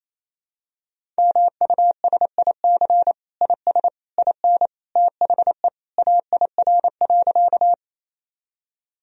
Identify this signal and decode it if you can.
Morse code, heard as a keyed tone, about 28 wpm, 700 Hz. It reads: MUSIC IS IN THE AIR.
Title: Sentences from Top 1000 Words - Review Part 1 - 28wpm